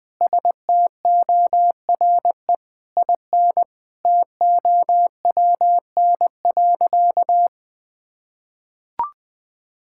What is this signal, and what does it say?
Morse code, heard as a keyed tone, about 21 wpm, 700 Hz. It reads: STORE IN TOWN.
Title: Sentences from Top 1000 Words - Review Part 5 - 20wpm